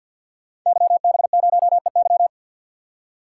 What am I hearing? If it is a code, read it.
Morse code, heard as a keyed tone, about 51 wpm, 700 Hz. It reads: YB0EY
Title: Code Groups - 5 Letter:Numbers - 50wpm